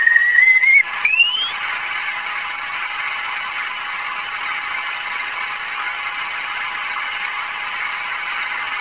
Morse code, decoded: none